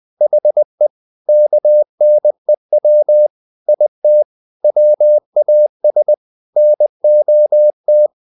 Morse code, heard as a keyed tone, about 20 wpm, 600 Hz. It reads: HE KNEW IT WAS NOT
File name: the old man and the sea / the old man and the sea part 1 20wpm